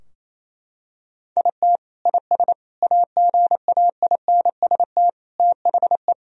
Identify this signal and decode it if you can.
Morse code, heard as a keyed tone, about 28 wpm, 700 Hz. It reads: IT IS AGAINST THE
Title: Sentences from Top 1000 Words - Review Part 3 - 28wpm